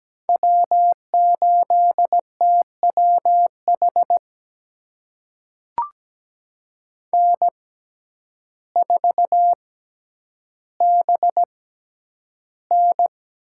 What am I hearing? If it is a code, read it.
Morse code, heard as a keyed tone, about 17 wpm, spaced as for 9 wpm, 700 Hz. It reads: W8TWH N4BN
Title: USA Call Signs - 6x Character Spacing - 17wpm